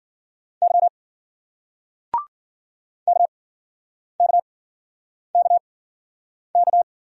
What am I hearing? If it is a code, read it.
Morse code, heard as a keyed tone, about 54 wpm, spaced as for 9 wpm, 700 Hz. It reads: Y KKKK